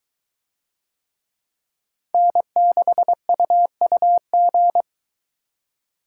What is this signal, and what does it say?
Morse code, heard as a keyed tone, about 22 wpm, 700 Hz. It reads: N6UUG